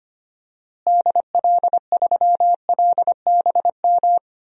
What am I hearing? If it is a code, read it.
Morse code, heard as a keyed tone, about 25 wpm, 700 Hz. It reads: DL3LBM
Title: DXCC-100 Call Signs - 6 Characters - Repeated 3x - 25wpm